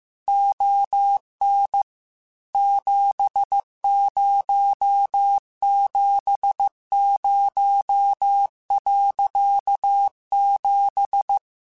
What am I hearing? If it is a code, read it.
Morse code, heard as a keyed tone, about 15 wpm, 800 Hz. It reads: ON 7070.7
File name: QSOs / QSOs-mp3s / kc7sac@13